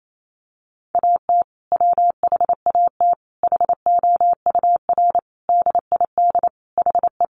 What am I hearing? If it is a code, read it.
Morse code, heard as a keyed tone, about 28 wpm, 700 Hz. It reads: AT WHAT HOUR DID HE
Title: Sentences from Top 700 Words 28wpm